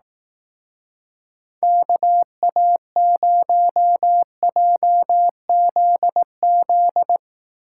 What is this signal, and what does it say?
Morse code, heard as a keyed tone, about 20 wpm, 700 Hz. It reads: KA0JZZ